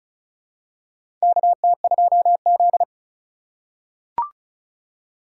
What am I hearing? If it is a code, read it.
Morse code, heard as a keyed tone, about 36 wpm, 700 Hz. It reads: KT2Z
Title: USA Call Signs - 3x Character Spacing - 35wpm